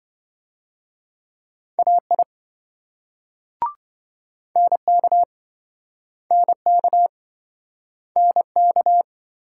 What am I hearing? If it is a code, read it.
Morse code, heard as a keyed tone, about 27 wpm, 700 Hz. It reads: AI NK NK NK